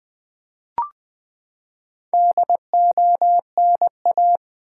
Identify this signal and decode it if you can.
Morse code, heard as a keyed tone, about 20 wpm, 700 Hz. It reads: DONA
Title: Build A Word - 4 Letters At A Time - 8-Letter Words - 20wpm